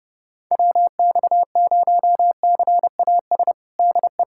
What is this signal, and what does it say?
Morse code, heard as a keyed tone, about 30 wpm, 700 Hz. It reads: WX0CAS DE
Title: USA Callsign DE Callsign - 6 Characters - 30wpm